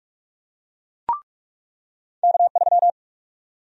Morse code, heard as a keyed tone, about 45 wpm, 700 Hz. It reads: K3
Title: USA Call Sign Prefixes - 45wpm